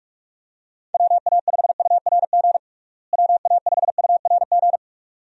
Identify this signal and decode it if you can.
Morse code, heard as a keyed tone, about 46 wpm, 700 Hz. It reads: WA5URG WA5URG
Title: USA Call Signs - 6 Characters - Repeated 2x - 45wpm